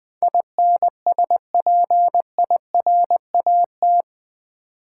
Morse code, heard as a keyed tone, about 20 wpm, 700 Hz. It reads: INSPIRAT
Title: Build A Word - 4 Letters At A Time - 11-Letter Words - 20wpm